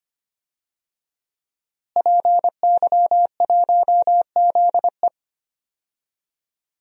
Morse code, heard as a keyed tone, about 25 wpm, 700 Hz. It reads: PY1ZE
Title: DXCC-100 Call Signs - 5 Characters - Repeated 3x - 25wpm